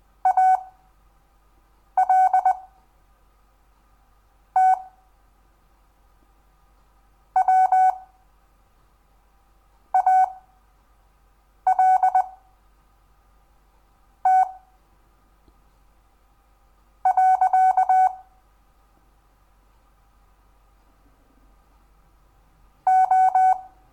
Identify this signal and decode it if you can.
Morse code, heard as a keyed tone, about 20 wpm, spaced as for 4 wpm, 750 Hz. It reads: ALT WALT. O
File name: cw / Lesson10 REVIEW